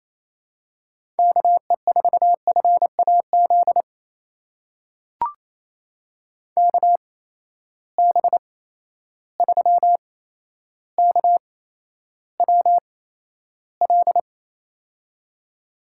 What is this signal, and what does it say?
Morse code, heard as a keyed tone, about 28 wpm, spaced as for 12 wpm, 700 Hz. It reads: KE4FAZ KB3KWL
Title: USA Call Signs - 6 Characters - 8x Character Spacing - 28wpm